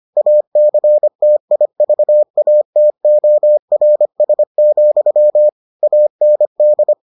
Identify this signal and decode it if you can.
Morse code, heard as a keyed tone, about 25 wpm, 600 Hz. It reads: ACTIVATORS, AND